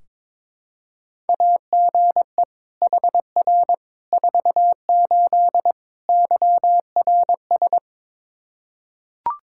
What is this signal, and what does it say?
Morse code, heard as a keyed tone, about 22 wpm, 700 Hz. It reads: AGE HR 48 YRS
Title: Practice QSO - Exchange 3 - More Info - 22wpm